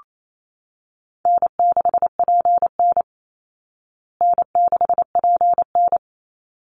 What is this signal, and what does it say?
Morse code, heard as a keyed tone, about 28 wpm, 700 Hz. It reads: N6PN N6PN
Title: Easy Call Signs Repeated 2x - 28wpm